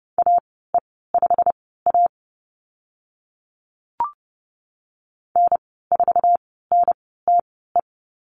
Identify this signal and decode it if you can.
Morse code, heard as a keyed tone, about 30 wpm, spaced as for 17 wpm, 700 Hz. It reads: AE5A N4NTE
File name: Lesson 15g - Number 5 - Call Signs 30wpm